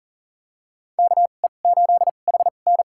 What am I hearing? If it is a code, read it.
Morse code, heard as a keyed tone, about 40 wpm, spaced as for 29 wpm, 700 Hz. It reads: KE8HN
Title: USA Call Signs - 5 Characters - 2x Character Spacing - 40wpm